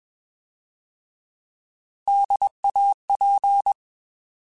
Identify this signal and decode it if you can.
Morse code, heard as a keyed tone, about 21 wpm, 800 Hz. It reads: DAP
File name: Alphabetical / Alphabetical mp3s / Ds @20